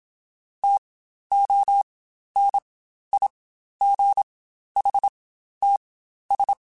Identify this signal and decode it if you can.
Morse code, heard as a keyed tone, about 27 wpm, spaced as for 12 wpm, 800 Hz. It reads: TONIGHTS
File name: Spaced Characters / Spaced Characters-mp3s / Weather4 14 sp @25